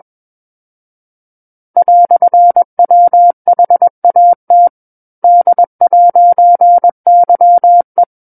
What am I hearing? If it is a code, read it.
Morse code, heard as a keyed tone, about 21 wpm, 700 Hz. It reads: "WHAT D'YE
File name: Book One / 12-Chapter 12 - 21 WPM